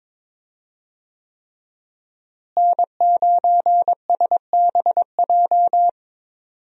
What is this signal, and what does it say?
Morse code, heard as a keyed tone, about 22 wpm, 700 Hz. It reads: N9SBJ
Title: USA Call Signs - 5 Characters - Repeated 3x - 22wpm